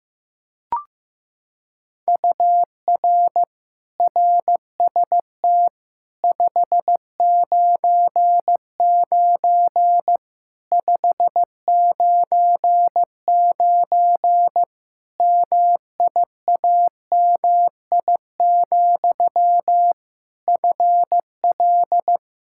Morse code, heard as a keyed tone, about 15 wpm, 700 Hz. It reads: UR RST 599 599 MIAMI, FL
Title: Practice QSO - Exchange 1 - Beginning - 15wpm